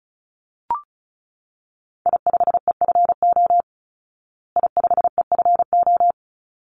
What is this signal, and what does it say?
Morse code, heard as a keyed tone, about 34 wpm, 700 Hz. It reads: I5EFO I5EFO
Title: Call Signs Repeated 3x - 35wpm